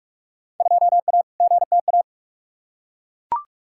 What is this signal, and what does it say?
Morse code, heard as a keyed tone, about 41 wpm, 700 Hz. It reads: 2A GTA
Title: Field Day Exchanges Encore - 45wpm